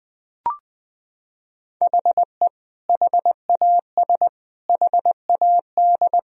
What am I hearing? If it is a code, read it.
Morse code, heard as a keyed tone, about 20 wpm, 700 Hz. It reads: HE HAS HAD
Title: Sentences from Top 1000 Words - Review Part 1 - 20wpm